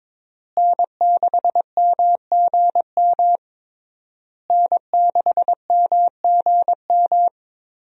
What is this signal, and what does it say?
Morse code, heard as a keyed tone, about 22 wpm, 700 Hz. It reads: N6MGM N6MGM
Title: USA Call Signs - 5 Characters - Repeated 3x - 22wpm